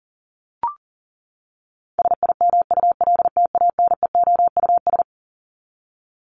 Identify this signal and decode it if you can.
Morse code, heard as a keyed tone, about 37 wpm, 700 Hz. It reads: SIMULTANEOUS
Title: LY Suffix - Top 100 Words with Base Words 40wpm